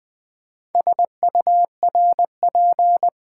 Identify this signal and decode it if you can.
Morse code, heard as a keyed tone, about 20 wpm, 700 Hz. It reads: SURP
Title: Build A Word - 4 Letters At A Time - 10-Letter Words - 20wpm